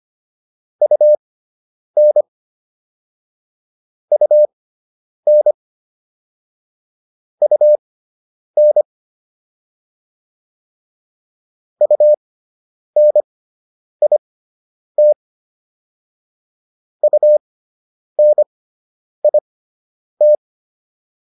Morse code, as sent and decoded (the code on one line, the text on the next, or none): ..- -. / ..- -. / ..- -. / ..- -. .. - / ..- -. .. -
UN UN UN UNIT UNIT